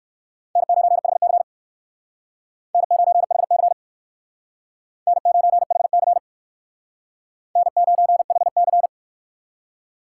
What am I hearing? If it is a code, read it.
Morse code, heard as a keyed tone, about 64 wpm, 700 Hz. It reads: N9HC N9HC N9HC N9HC